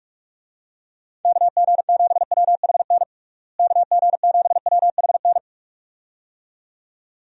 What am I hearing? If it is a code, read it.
Morse code, heard as a keyed tone, about 46 wpm, 700 Hz. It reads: KG7WHN KG7WHN
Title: USA Call Signs - 6 Characters - Repeated 2x - 45wpm